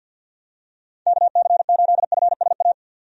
Morse code, heard as a keyed tone, about 52 wpm, 700 Hz. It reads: KC8FSA
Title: USA Call Signs - 6 Characters - 50wpm